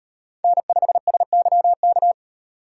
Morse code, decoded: N5SYK

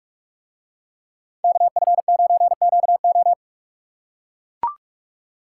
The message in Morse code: KF9QO

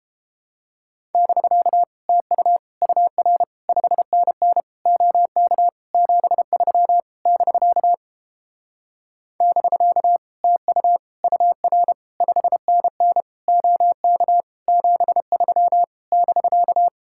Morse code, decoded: <BK> TU UR 5NN OK 73 <BK> <BK> TU UR 5NN OK 73 <BK>

